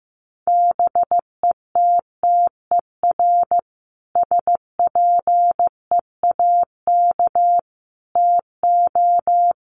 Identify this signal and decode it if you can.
Morse code, heard as a keyed tone, about 15 wpm, 700 Hz. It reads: BETTER SPEAK TO
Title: Sentences from Top 1100 Words 15wpm